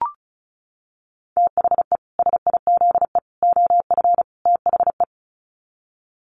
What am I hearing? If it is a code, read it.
Morse code, heard as a keyed tone, about 34 wpm, 700 Hz. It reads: THE SIZE OF THE
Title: Sets of 4 Words 35wpm